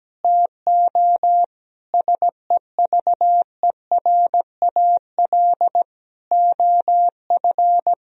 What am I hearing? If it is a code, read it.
Morse code, heard as a keyed tone, about 17 wpm, 700 Hz. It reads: TO SEVERAL OF
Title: Sentences from Top 1000 Words - Review Part 2 - 17wpm